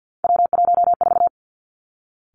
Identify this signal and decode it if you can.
Morse code, heard as a keyed tone, about 50 wpm, 700 Hz. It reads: W14
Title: Three Challenging Letter-Number Combinations 50wpm